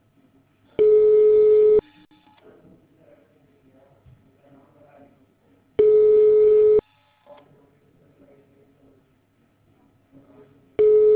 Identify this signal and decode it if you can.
Ringback tone (Europe, 425 Hz).